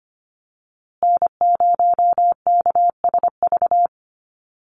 Morse code, heard as a keyed tone, about 25 wpm, 700 Hz. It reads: N0KSV
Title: USA Call Signs Repeated 2x 25wpm